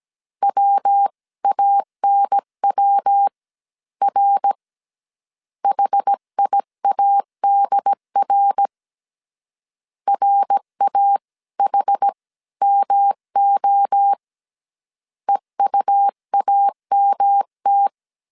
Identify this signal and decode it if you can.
Morse code, heard as a keyed tone, about 17 wpm, spaced as for 14 wpm, 800 Hz. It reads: WANWR HIADR RAHMO EUAMT